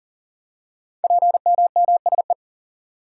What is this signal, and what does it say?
Morse code, heard as a keyed tone, about 41 wpm, 700 Hz. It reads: PMMSE
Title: Code Groups - 5 Letters - 40wpm